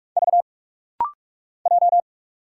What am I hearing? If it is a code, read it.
Morse code, heard as a keyed tone, about 44 wpm, 700 Hz. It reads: V J
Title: Single Letter-Number Alt - Rapid-Fire 45wpm